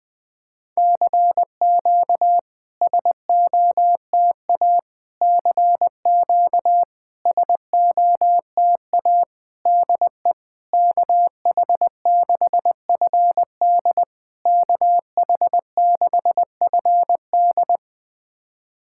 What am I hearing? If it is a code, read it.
Morse code, heard as a keyed tone, about 20 wpm, 690 Hz. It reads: CQ SOTA CQ SOTA DE KH6FD KH6FD
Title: SOTA QSO Practice 20wpm